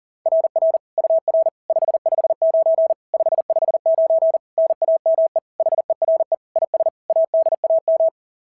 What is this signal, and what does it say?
Morse code, heard as a keyed tone, about 37 wpm, 690 Hz. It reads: RR UR 559 559 NAME HERE IS ADAM
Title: SOTA QSO Practice 40wpm